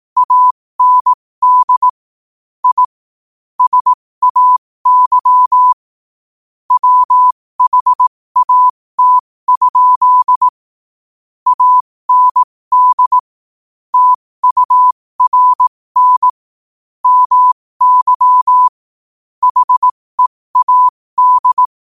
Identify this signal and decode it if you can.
Morse code, heard as a keyed tone, about 18 wpm, spaced as for 15 wpm, 1000 Hz. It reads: AND I SAY WHAT? AND TURN MY HEAD